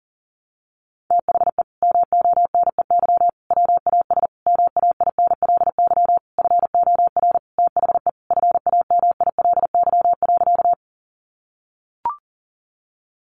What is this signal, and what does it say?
Morse code, heard as a keyed tone, about 40 wpm, 700 Hz. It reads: THE MONEY WAS MAINLY FOR THE FAMILY.
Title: Sentences from Top 1400 Words 40wpm